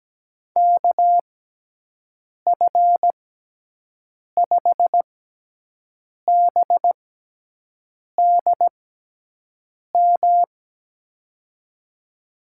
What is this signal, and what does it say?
Morse code, heard as a keyed tone, about 17 wpm, spaced as for 6 wpm, 700 Hz. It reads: KF5BDM
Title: USA Call Signs - 6 Characters - 6x Character Spacing - 17wpm